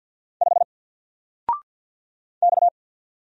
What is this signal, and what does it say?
Morse code, heard as a keyed tone, about 50 wpm, 700 Hz. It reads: 5 X
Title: Single Character (Letter:Number:Punctuation:Prosign) 50wpm